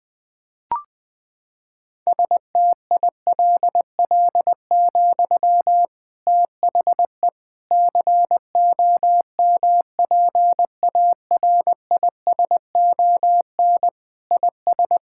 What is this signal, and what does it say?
Morse code, heard as a keyed tone, about 20 wpm, 700 Hz. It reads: STILL, THE COMPARISON IS
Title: Sentences from Top 1600 Words 20wpm